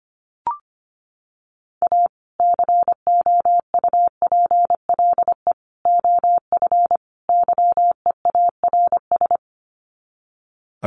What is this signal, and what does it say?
Morse code, heard as a keyed tone, about 25 wpm, 700 Hz. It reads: A COUPLE OF YEARS
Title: Sets of 4 Words 25wpm